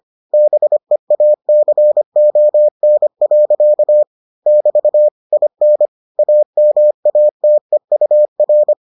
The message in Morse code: BEACON. <BT> IN AMATEUR

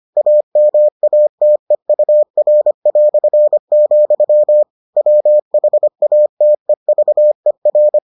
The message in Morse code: AMATEUR", WHATEVER

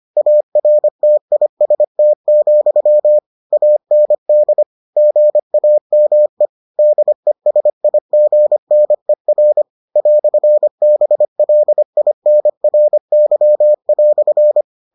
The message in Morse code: ARTIST, AND GAME DESIGNER "BLINRY"

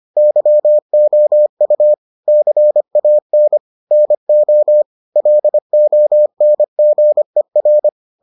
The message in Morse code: YOU CAN NO LONGER

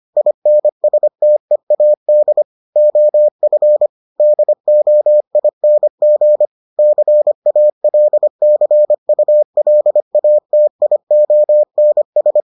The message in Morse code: INSTEAD OF DOING CALCULATIONS

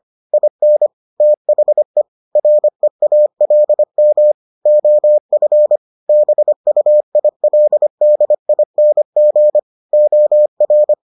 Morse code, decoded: IN THE REALM OF BUILDING OR

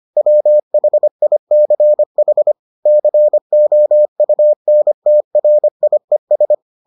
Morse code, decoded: WHICH COUNTRIES

